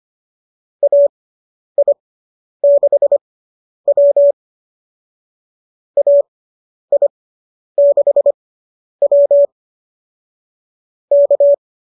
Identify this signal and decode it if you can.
Morse code, heard as a keyed tone, about 25 wpm, spaced as for 10 wpm, 590 Hz. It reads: AI6W AI6W K